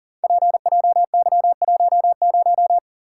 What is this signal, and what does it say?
Morse code, heard as a keyed tone, about 38 wpm, 700 Hz. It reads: PJY10
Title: Code Groups - 5 Letter:Numbers - 40wpm